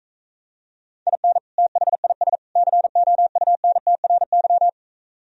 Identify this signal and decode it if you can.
Morse code, heard as a keyed tone, about 42 wpm, 700 Hz. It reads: IN THIS COUNTRY